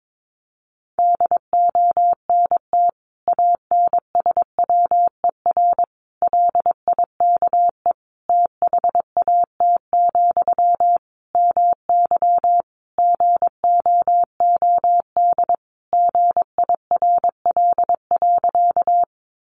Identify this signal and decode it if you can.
Morse code, heard as a keyed tone, about 22 wpm, 700 Hz. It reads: DONT ANSWER LIKE THAT, MY GOOD GIRL.
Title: Sentences from Top 400 Words 22wpm